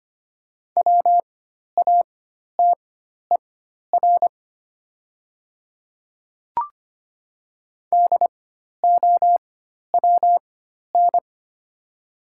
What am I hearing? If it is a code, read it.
Morse code, heard as a keyed tone, about 25 wpm, spaced as for 12 wpm, 700 Hz. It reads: WATER DOWN
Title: 4x Character Spacing - Top 100 Words - 25wpm